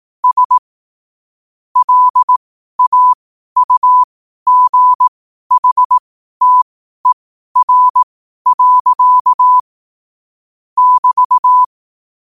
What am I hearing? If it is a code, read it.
Morse code, heard as a keyed tone, about 18 wpm, spaced as for 12 wpm, 1000 Hz. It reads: S LAUGHTER. <BT>
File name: qotdfiles / 12wpm